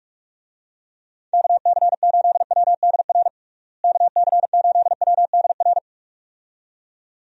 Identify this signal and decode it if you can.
Morse code, heard as a keyed tone, about 46 wpm, 700 Hz. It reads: KC8WDR KC8WDR
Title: USA Call Signs - 6 Characters - Repeated 2x - 45wpm